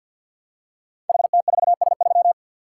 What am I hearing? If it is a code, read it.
Morse code, heard as a keyed tone, about 51 wpm, 700 Hz. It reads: HT4S3
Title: Code Groups - 5 Letter:Numbers - 50wpm